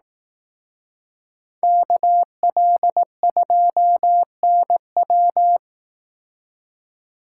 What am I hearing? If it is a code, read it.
Morse code, heard as a keyed tone, about 18 wpm, 700 Hz. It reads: KL2NW
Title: Speed Racing - USA Call Signs 15wpm